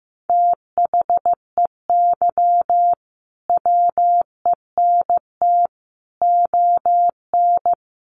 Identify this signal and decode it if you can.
Morse code, heard as a keyed tone, about 15 wpm, 700 Hz. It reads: THEY WENT ON
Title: Sentences from Top 1400 Words 15wpm